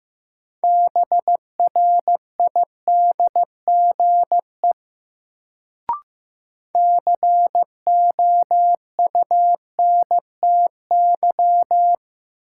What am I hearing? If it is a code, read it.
Morse code, heard as a keyed tone, about 15 wpm, 700 Hz. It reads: BRIDGE COUNTY
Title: Sentences from Top 1200 Words - New Word Drill Alt - 15wpm